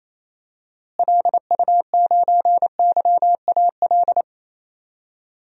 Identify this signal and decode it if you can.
Morse code, heard as a keyed tone, about 27 wpm, 700 Hz. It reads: LU9YAL
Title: DXCC-100 Call Signs - 6 Characters - Repeated 3x - 28wpm